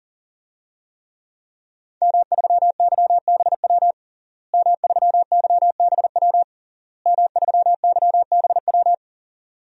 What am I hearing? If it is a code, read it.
Morse code, heard as a keyed tone, about 41 wpm, 700 Hz. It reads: M3YBW M3YBW M3YBW
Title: DXCC-100 Call Signs - 5 Characters - Repeated 3x - 40wpm